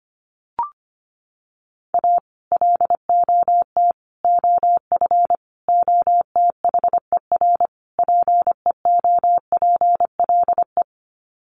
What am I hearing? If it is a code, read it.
Morse code, heard as a keyed tone, about 25 wpm, 700 Hz. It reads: A LOT OF OTHER PEOPLE
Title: Sets of 5 Words Encore2 25wpm